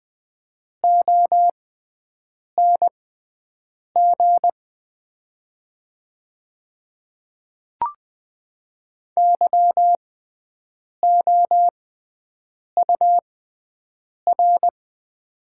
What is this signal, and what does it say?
Morse code, heard as a keyed tone, about 20 wpm, spaced as for 7 wpm, 700 Hz. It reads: ONG YOUR
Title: 6x Character Spacing - Top 100 Words - 20wpm